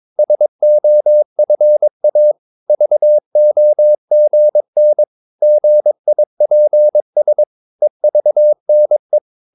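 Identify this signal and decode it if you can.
Morse code, heard as a keyed tone, about 22 wpm, 600 Hz. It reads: SOFA VOGN GIPS EVNE
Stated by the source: Morse practice of short Danish words at 22 wpm, from CW med Gnister.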